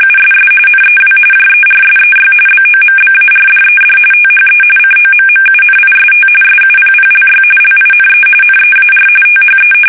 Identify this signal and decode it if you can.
RTTY, about 45.45 baud. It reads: USS NEW JERSEY BOMBARDED THE COMMUNIST SUPPLY LINE NEAR THE
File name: fsk850 / battleshipnj